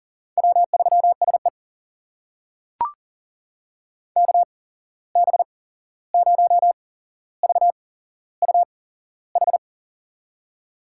Morse code, heard as a keyed tone, about 40 wpm, spaced as for 15 wpm, 700 Hz. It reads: W3SE KB0VUH
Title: USA Call Signs - 8x Character Spacing - 40wpm